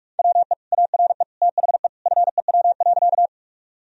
Morse code, heard as a keyed tone, about 46 wpm, 700 Hz. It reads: WE ARE THE FEW.
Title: Sentences from Top 1000 Words - Review Part 1 - 45wpm